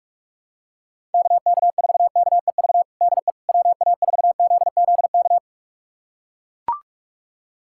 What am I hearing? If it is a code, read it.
Morse code, heard as a keyed tone, about 41 wpm, 700 Hz. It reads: KK4KEV DE WA4ZZK
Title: USA Callsign DE Callsign - 6 Characters - 45wpm